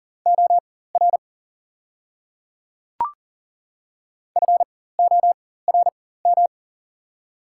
Morse code, heard as a keyed tone, about 37 wpm, spaced as for 18 wpm, 700 Hz. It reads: OR FORM
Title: 4x Character Spacing - Top 200 Words - 40wpm